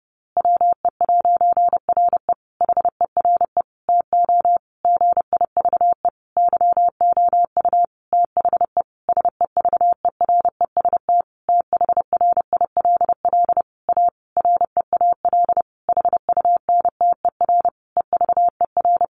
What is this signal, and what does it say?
Morse code, heard as a keyed tone, about 30 wpm, 700 Hz. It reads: WE'RE HERE TO GIVE YOU THE SEVEREST THRILL A REAL HUNTER EVER
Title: A Sound Of Thunder 30wpm